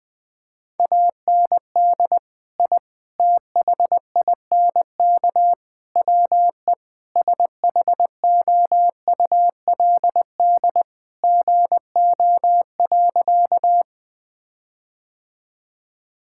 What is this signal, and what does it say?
Morse code, heard as a keyed tone, about 20 wpm, 700 Hz. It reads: AND I THINK WE SHOULD GO.
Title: Sentences with 4-Word Phrases - 20wpm